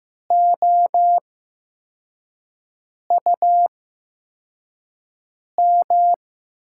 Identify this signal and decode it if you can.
Morse code, heard as a keyed tone, about 15 wpm, spaced as for 4 wpm, 700 Hz. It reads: OUM
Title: USA Call Signs - 8x Character Spacing - 15wpm